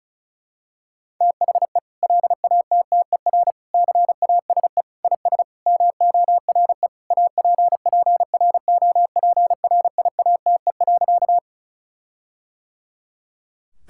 Morse code, heard as a keyed tone, about 34 wpm, 700 Hz. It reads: THE LATTER CASE IS MORE APPROPRIATE.
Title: Sentences from Top 1600 Words 35wpm